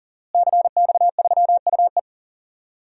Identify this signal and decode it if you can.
Morse code, heard as a keyed tone, about 40 wpm, 700 Hz. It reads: CX3UE
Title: DXCC-100 Call Signs - 5 Characters - Repeated 3x - 40wpm